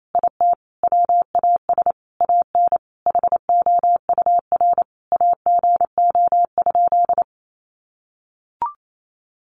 Morse code, heard as a keyed tone, about 28 wpm, 700 Hz. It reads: IT WAS AN HOUR AGO?
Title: Sentences from Top 1300 Words 28wpm